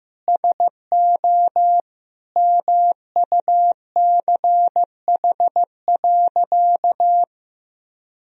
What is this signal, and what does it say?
Morse code, heard as a keyed tone, about 15 wpm, 700 Hz. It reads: SO MUCH.
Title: Sentences from Top 1000 Words - Review Part 4 - 15wpm